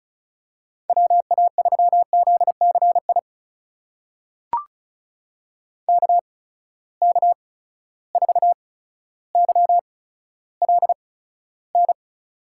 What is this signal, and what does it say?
Morse code, heard as a keyed tone, about 34 wpm, spaced as for 15 wpm, 700 Hz. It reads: WA3ZCI KK4YLN
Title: USA Call Signs - 6 Characters - 8x Character Spacing - 35wpm